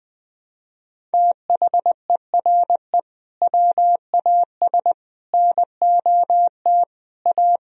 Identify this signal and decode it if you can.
Morse code, heard as a keyed tone, about 21 wpm, 700 Hz. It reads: THERE WAS NOT A
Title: Sentences from Top 1000 Words - Review Part 1 - 20wpm